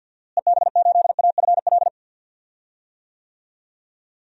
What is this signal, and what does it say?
Morse code, heard as a keyed tone, about 53 wpm, 700 Hz. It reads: EB8AFL
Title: DXCC-100 Call Signs - 6 Characters - Repeated 2x - 50wpm